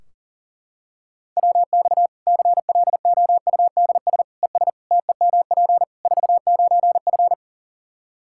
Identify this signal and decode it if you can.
Morse code, heard as a keyed tone, about 37 wpm, 700 Hz. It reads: WX CLOUDS ES TEMP 49F
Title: K1USN SST QSO Practice 40wpm